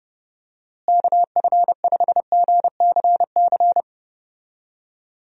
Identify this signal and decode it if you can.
Morse code, heard as a keyed tone, about 30 wpm, 700 Hz. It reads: KF5GCC
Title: USA Call Signs - 6 Characters - Repeated 2x - 30wpm